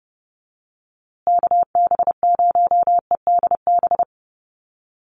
Morse code, heard as a keyed tone, about 30 wpm, 700 Hz. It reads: KB0EDB
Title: USA Call Signs Repeated 3x 30wpm